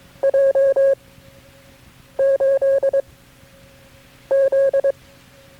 Morse code, heard as a keyed tone, about 22 wpm, 540 Hz. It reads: J 8 Z
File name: KB6MT-tape3-side1